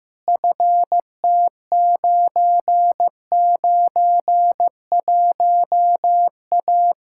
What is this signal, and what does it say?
Morse code, heard as a keyed tone, about 16 wpm, 700 Hz. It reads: FT991A
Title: Practice QSO - Exchange 2 - Info - 15wpm